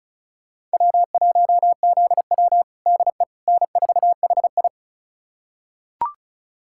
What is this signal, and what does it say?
Morse code, heard as a keyed tone, about 34 wpm, 700 Hz. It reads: W1ZW DE N4HI
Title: USA Callsign DE Callsign - 4 Characters - 35wpm